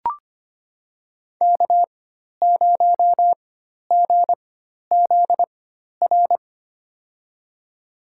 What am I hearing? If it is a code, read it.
Morse code, heard as a keyed tone, about 26 wpm, spaced as for 12 wpm, 700 Hz. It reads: K0GZR
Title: USA Call Signs - 5 Characters - 4x Character Spacing - 25wpm